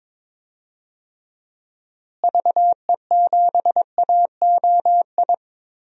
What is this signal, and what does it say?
Morse code, heard as a keyed tone, about 21 wpm, 700 Hz. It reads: VE7AOI
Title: DXCC-100 Call Signs - 6 Characters - Repeated 3x - 22wpm